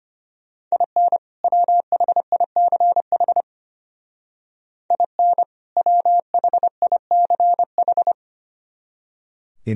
Morse code, heard as a keyed tone, about 30 wpm, 700 Hz. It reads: IN WHICH IN WHICH
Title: Speed-Racing - Sets of 2 Words 25wpm